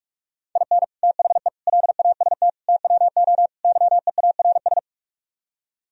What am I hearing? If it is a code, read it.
Morse code, heard as a keyed tone, about 48 wpm, 700 Hz. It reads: IN THE LAST TWO YEARS